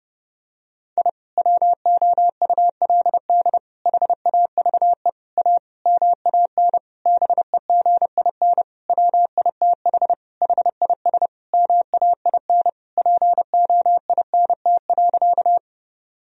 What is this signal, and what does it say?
Morse code, heard as a keyed tone, about 32 wpm, 700 Hz. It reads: I WOULD HAVE A MAN BEGIN WITH HIS MAIN POINT.
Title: Sentences from Top 1000 Words - Review Part 2 - 30wpm